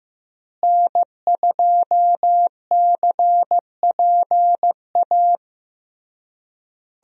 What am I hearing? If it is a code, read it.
Morse code, heard as a keyed tone, about 16 wpm, 700 Hz. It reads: N2CPA
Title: USA Call Signs - 5 Characters - Repeated 3x - 15wpm